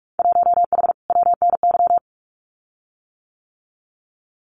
Morse code, heard as a keyed tone, about 44 wpm, 700 Hz. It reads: 1H WNY
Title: Winter Field Day Exchanges 45wpm